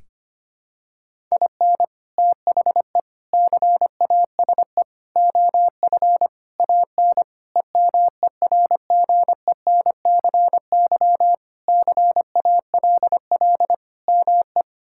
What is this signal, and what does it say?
Morse code, heard as a keyed tone, about 25 wpm, 700 Hz. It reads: IN THE CASE OF AN EMERGENCY CALL ME
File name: practice / 25wpm / Sentences with 3-Word Phrases - 25wpm